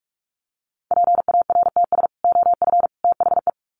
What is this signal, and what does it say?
Morse code, heard as a keyed tone, about 44 wpm, 700 Hz. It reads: PARTS OF THE
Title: Sets of 3 Words 45wpm